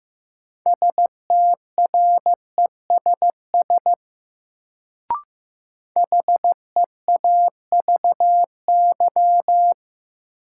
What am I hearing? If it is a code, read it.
Morse code, heard as a keyed tone, about 15 wpm, 700 Hz. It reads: STRESS HEAVY
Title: Sentences from Top 1000 Words - Review Part 6 - New Word Drill - 15wpm